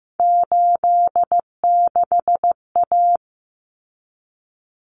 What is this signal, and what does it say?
Morse code, heard as a keyed tone, about 15 wpm, 700 Hz. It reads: 86A
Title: Common Cut Numbers - 3-Digit 15wpm